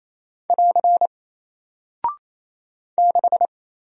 Morse code, heard as a keyed tone, about 27 wpm, 700 Hz. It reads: <AR> 6
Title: Single Character (Letter:Number:Punctuation:Prosign) 28wpm